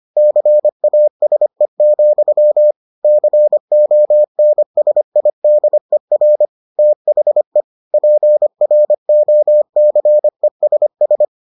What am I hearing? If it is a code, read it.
Morse code, heard as a keyed tone, about 25 wpm, 600 Hz. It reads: CASE, CONSIDER THE PROCESS